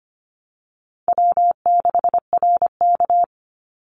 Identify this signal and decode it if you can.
Morse code, heard as a keyed tone, about 25 wpm, 700 Hz. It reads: W6RK
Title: Easy Call Signs - 25wpm